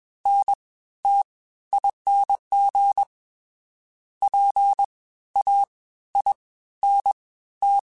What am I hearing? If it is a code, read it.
Morse code, heard as a keyed tone, about 21 wpm, spaced as for 14 wpm, 800 Hz. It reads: NTING PAINT